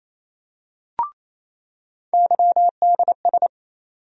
Morse code, heard as a keyed tone, about 28 wpm, 700 Hz. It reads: YDS
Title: USA Call Sign Postfixes - 3 Letters 28wpm